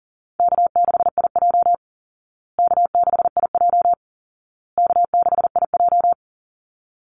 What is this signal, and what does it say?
Morse code, heard as a keyed tone, about 40 wpm, 700 Hz. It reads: K6IJ K6IJ K6IJ
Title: Easy Call Signs Repeated 3x - 40wpm